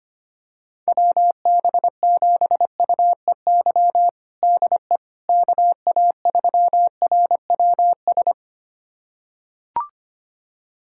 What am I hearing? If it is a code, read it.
Morse code, heard as a keyed tone, about 25 wpm, 700 Hz. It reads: WB7UEY DE KA3RWS
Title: USA Callsign DE Callsign - 6 Characters - 25wpm